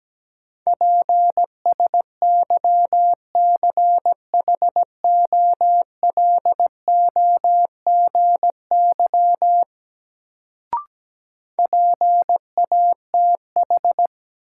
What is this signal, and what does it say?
Morse code, heard as a keyed tone, about 17 wpm, 700 Hz. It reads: PSYCHOLOGY PATH
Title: Sentences from Top 1400 Words - New Word Drill - 17wpm